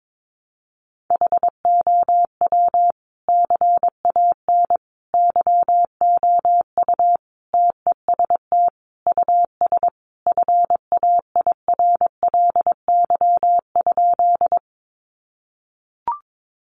Morse code, heard as a keyed tone, about 22 wpm, 700 Hz. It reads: HOW CAN YOU TEST US FAIRLY?
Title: Sentences from Top 1100 Words 22wpm